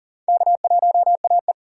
Morse code, heard as a keyed tone, about 40 wpm, 700 Hz. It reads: K1AE
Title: K1USN SST QSO Practice - 40wpm